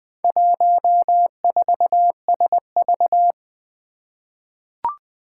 Morse code, heard as a keyed tone, about 20 wpm, 700 Hz. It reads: 14SV
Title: Four Challenging Letter-Number Combinations 20wpm